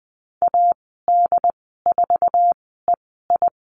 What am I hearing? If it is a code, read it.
Morse code, heard as a keyed tone, about 19 wpm, spaced as for 14 wpm, 700 Hz. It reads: AD4EI
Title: 167 Lesson 27g - Number 7 - Call Signs 20wpm